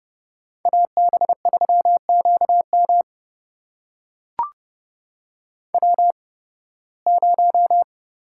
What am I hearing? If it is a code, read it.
Morse code, heard as a keyed tone, about 29 wpm, spaced as for 19 wpm, 700 Hz. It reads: AB3QM W0
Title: USA Call Signs - 5 Characters - 8x Character Spacing - 30wpm